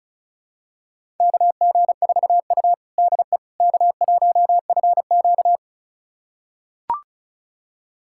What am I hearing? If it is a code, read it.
Morse code, heard as a keyed tone, about 36 wpm, 700 Hz. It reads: KG4U DE K1FQ
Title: USA Callsign DE Callsign - 4 Characters - 35wpm